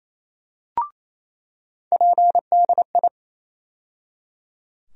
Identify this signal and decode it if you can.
Morse code, heard as a keyed tone, about 28 wpm, 700 Hz. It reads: PDI
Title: USA Call Sign Postfixes - 3 Letters 28wpm